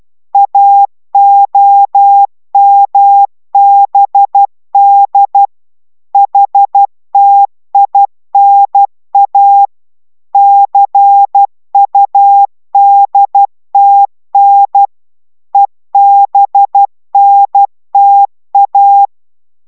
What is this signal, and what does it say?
Morse code, heard as a keyed tone, about 12 wpm, 800 Hz. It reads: AOMBD HTINA CUDTN EBNTA